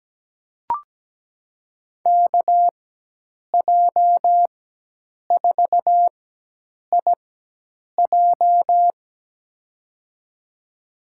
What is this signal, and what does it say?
Morse code, heard as a keyed tone, about 17 wpm, spaced as for 8 wpm, 700 Hz. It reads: KJ4IJ
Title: USA Call Signs - 5 Characters - 4x Character Spacing - 17wpm